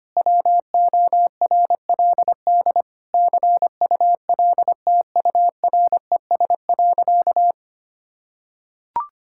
Morse code, heard as a keyed tone, about 25 wpm, 700 Hz. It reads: WORLD CULTURES.
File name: practice / 25wpm / Difficult Sentences with Hints - 25wpm